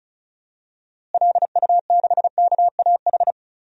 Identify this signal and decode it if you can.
Morse code, heard as a keyed tone, about 34 wpm, 700 Hz. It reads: LU6KAH
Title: DXCC-100 Call Signs - 6 Characters - Repeated 2x - 35wpm